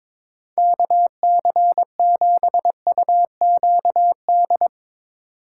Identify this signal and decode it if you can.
Morse code, heard as a keyed tone, about 22 wpm, 700 Hz. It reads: KC7UQD
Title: Speed Racing - USA 2x3 Call Signs 22wpm